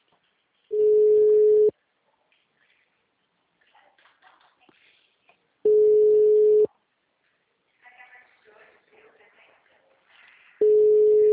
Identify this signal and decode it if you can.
Ringback tone (Europe, 425 Hz), with speech.